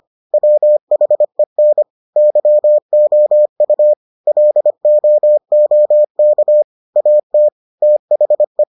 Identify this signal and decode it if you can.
Morse code, heard as a keyed tone, about 25 wpm, 600 Hz. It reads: WHEN YOU LOOK AT THE